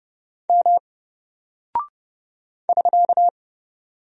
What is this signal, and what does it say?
Morse code, heard as a keyed tone, about 30 wpm, 700 Hz. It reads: M <SK>
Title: Single Character (Letter:Number:Punctuation:Prosign) Alt 30wpm